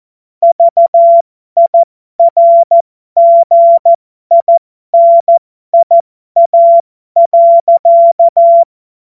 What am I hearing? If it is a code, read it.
Morse code, heard as a keyed tone, about 14 wpm, spaced as for 12 wpm, 680 Hz. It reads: VIRGINIA.
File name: Morsekurs / Witze